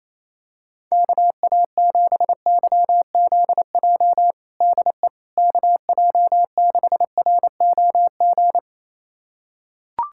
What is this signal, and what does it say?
Morse code, heard as a keyed tone, about 28 wpm, 700 Hz. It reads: KA7YZJ DE KJ6ROG
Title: USA Callsign DE Callsign - 6 Characters - 28wpm